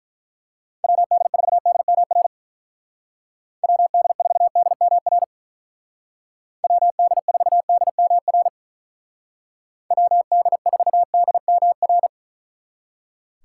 Morse code, heard as a keyed tone, about 43 wpm, 700 Hz. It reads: WD4DMR WD4DMR WD4DMR WD4DMR
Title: Speed Racing - USA 2x3 Call Signs 35wpm